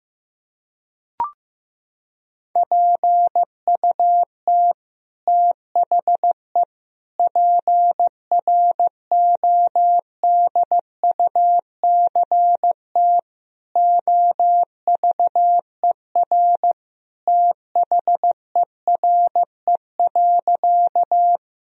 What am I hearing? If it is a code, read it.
Morse code, heard as a keyed tone, about 15 wpm, 700 Hz. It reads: PUT THE PRODUCT OVER THERE.
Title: Sentences from Top 1000 Words - Review Part 2 - 15wpm